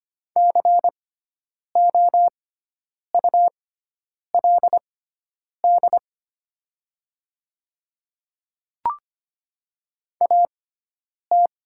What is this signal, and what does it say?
Morse code, heard as a keyed tone, about 25 wpm, spaced as for 9 wpm, 700 Hz. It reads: COULD AT
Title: 6x Character Spacing - Top 500 Words - 25wpm